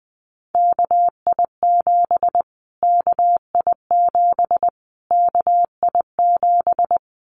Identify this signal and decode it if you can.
Morse code, heard as a keyed tone, about 20 wpm, 700 Hz. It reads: KI7 KI7 KI7
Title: Callsign Prefixes for Top 25 DXCC - Repeated 3x - 20wpm